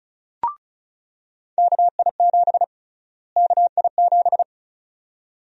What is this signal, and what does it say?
Morse code, heard as a keyed tone, about 34 wpm, 700 Hz. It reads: KI7 KI7
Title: USA Call Sign Prefixes - Repeated 2x - 35wpm